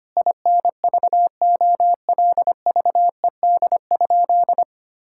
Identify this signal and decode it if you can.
Morse code, heard as a keyed tone, about 25 wpm, 700 Hz. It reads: INVOLVED?
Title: Sentences from Top 1000 Words - Review Part 5 - 25wpm